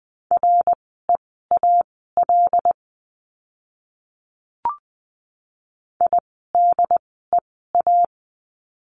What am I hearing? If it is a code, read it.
Morse code, heard as a keyed tone, about 20 wpm, spaced as for 14 wpm, 700 Hz. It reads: REAL IDEA
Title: 108 Lesson 18f - Letter m - Words 20wpm